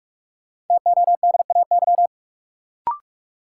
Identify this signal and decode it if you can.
Morse code, heard as a keyed tone, about 46 wpm, 700 Hz. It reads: TODAY
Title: Sets of 5 Words Encore - New Word Drill - 45wpm